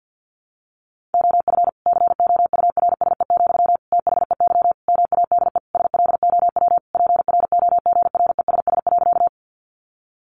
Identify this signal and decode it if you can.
Morse code, heard as a keyed tone, about 50 wpm, 700 Hz. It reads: OF COURSE, THEY MADE SLOW PROGRESS.
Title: Sentences from Top 1400 Words 50wpm